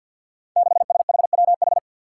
Morse code, heard as a keyed tone, about 53 wpm, 700 Hz. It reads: 6SHWH
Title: Five Challenging Letter-Number Combinations 50wpm